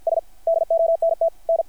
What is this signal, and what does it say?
Morse code, heard as a keyed tone, about 50 wpm, 660 Hz. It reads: S DONT N
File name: archived / CRSnet